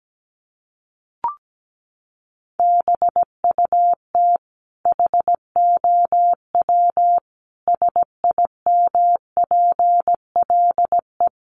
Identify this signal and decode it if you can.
Morse code, heard as a keyed tone, about 17 wpm, 700 Hz. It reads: BUT HOW SIMPLE
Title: Sentences from Top 400 Words 17wpm